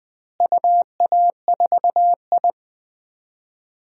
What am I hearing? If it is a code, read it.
Morse code, heard as a keyed tone, about 20 wpm, 700 Hz. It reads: UA4I
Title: DXCC-100 Call Signs - 4 Characters - Repeated 3x - 20wpm